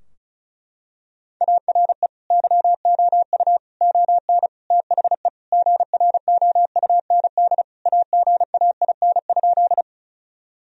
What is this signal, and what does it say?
Morse code, heard as a keyed tone, about 36 wpm, 700 Hz. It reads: ARE YOU ON THE GROUND AGAIN?
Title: Sentences from Top 1000 Words - Review Part 2 - 35wpm